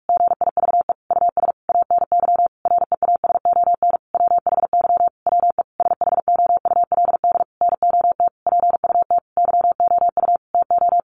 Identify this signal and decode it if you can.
Morse code, heard as a keyed tone, about 44 wpm, 700 Hz. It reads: GIVE US ANY REASON WHY WE SHOULD NOT PUT YOU TO
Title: Sentences from Top 1100 Words 45wpm